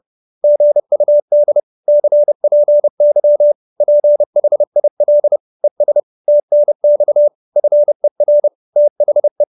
Morse code, heard as a keyed tone, about 30 wpm, 590 Hz. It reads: GUD CPY PHIL ES TNX FER THE